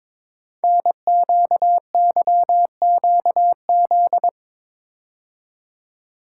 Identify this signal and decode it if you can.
Morse code, heard as a keyed tone, about 22 wpm, 700 Hz. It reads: NQYQZ
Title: Code Groups - 5 Letters - 22wpm